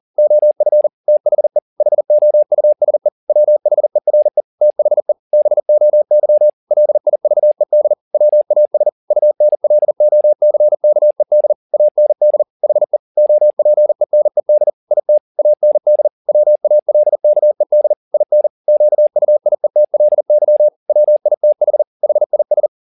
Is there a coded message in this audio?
Yes: Morse code from 0.2 to 22.7 s, heard as a keyed tone, about 40 wpm, 600 Hz. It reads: OF THE HOUSE WHERE THE BOY LIVED WAS UNLOCKED AND HE OPENED IT AND WALKED IN QUIETLY WITH HIS